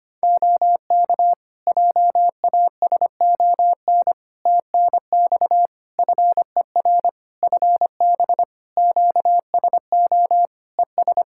Morse code, heard as a keyed tone, about 25 wpm, 700 Hz. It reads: OK JASON TNX FER FB QSO ES
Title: Practice QSO - Exchange 4 - Ending - 25wpm